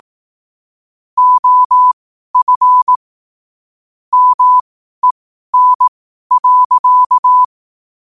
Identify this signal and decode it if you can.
Morse code, heard as a keyed tone, about 18 wpm, spaced as for 12 wpm, 1000 Hz. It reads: OF MEN.
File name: qotdfiles / 12wpm